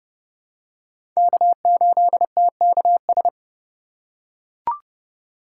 Morse code, heard as a keyed tone, about 30 wpm, 700 Hz. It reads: K8TKS
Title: USA Call Signs - 5 Characters - 4x Character Spacing - 30wpm